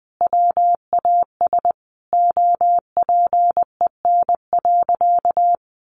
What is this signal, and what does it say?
Morse code, heard as a keyed tone, about 20 wpm, 700 Hz. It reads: WAS OPEN.
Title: Sentences from Top 1400 Words 20wpm